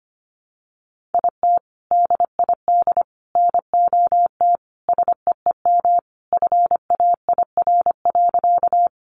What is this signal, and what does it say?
Morse code, heard as a keyed tone, about 25 wpm, 700 Hz. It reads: IT DID NOT SEEM FAIR.
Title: Sentences from Top 1300 Words 25wpm